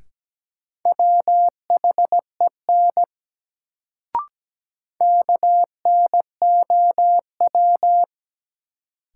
Morse code, heard as a keyed tone, about 17 wpm, 700 Hz. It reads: WHEN KNOW
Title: Sentences from Top 100 Words - New Word Drill - 17wpm